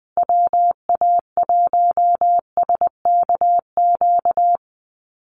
Morse code, heard as a keyed tone, about 20 wpm, 700 Hz. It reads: WA1SKQ
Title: Call Signs Repeated 2x - 20wpm